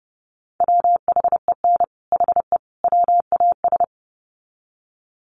Morse code, heard as a keyed tone, about 30 wpm, 700 Hz. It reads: WHEN HE WAS
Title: Sets of 3 Words 30wpm